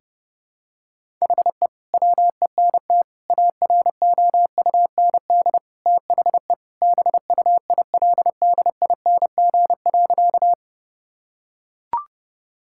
Morse code, heard as a keyed tone, about 30 wpm, 700 Hz. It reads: HE WENT AROUND THE BUILDING.
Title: Sentences from Top 1000 Words - Review Part 5 - 30wpm